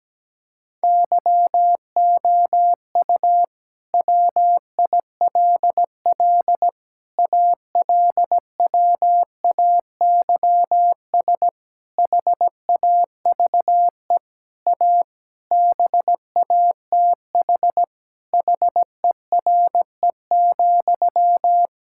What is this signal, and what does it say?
Morse code, heard as a keyed tone, about 17 wpm, 700 Hz. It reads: YOU WILL ALWAYS HAVE A BATH HERE,
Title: Sentences from Top 1600 Words 17wpm